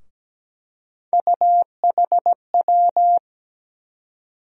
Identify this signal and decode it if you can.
Morse code, heard as a keyed tone, about 17 wpm, 700 Hz. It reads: UHW
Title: USA Call Sign Postfixes - 3 Letters 17wpm